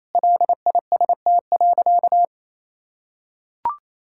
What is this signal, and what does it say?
Morse code, heard as a keyed tone, about 28 wpm, 700 Hz. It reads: LIST.
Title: Sentences from Top 1000 Words - Review Part 6 - 28wpm